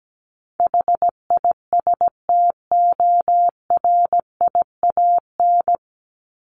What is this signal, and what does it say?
Morse code, heard as a keyed tone, about 17 wpm, 700 Hz. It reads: HISTORIAN
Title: Top 1751-2000 Words 17wpm